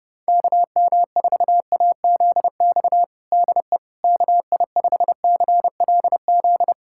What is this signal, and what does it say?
Morse code, heard as a keyed tone, about 30 wpm, 700 Hz. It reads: KM4AZX DE KI5CLZ
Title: USA Callsign DE Callsign - 6 Characters - 30wpm